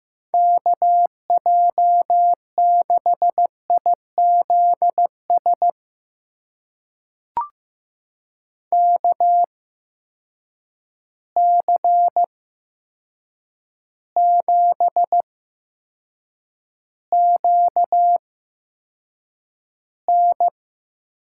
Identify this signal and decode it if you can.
Morse code, heard as a keyed tone, about 15 wpm, spaced as for 7 wpm, 700 Hz. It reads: KJ6IZS KC7QN
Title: USA Call Signs - 8x Character Spacing - 15wpm